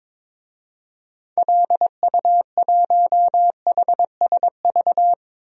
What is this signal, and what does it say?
Morse code, heard as a keyed tone, about 22 wpm, 700 Hz. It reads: LU1HSV
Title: DXCC-100 Call Signs - 6 Characters - 22wpm